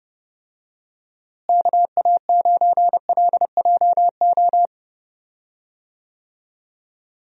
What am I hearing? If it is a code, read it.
Morse code, heard as a keyed tone, about 30 wpm, 700 Hz. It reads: KA9LJO
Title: USA Call Signs - 6 Characters - Repeated 2x - 30wpm